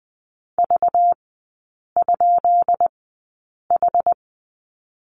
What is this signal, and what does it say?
Morse code, heard as a keyed tone, about 20 wpm, spaced as for 8 wpm, 700 Hz. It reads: V?H
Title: 198 Lesson 32e - Letter k - Three Characters 20wpm